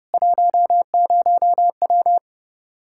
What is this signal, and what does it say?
Morse code, heard as a keyed tone, about 30 wpm, 700 Hz. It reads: 10W
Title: Practice QSO - Exchange 2 - Info - 30wpm